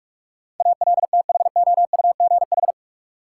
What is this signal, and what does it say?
Morse code, heard as a keyed tone, about 46 wpm, 700 Hz. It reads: ALTHOUGH
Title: Sentences from Top 600 Words - New Word Drill - 45wpm